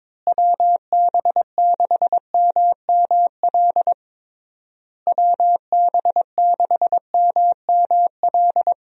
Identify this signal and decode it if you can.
Morse code, heard as a keyed tone, about 21 wpm, 700 Hz. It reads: WB6MML WB6MML
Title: USA Call Signs - 6 Characters - Repeated 3x - 22wpm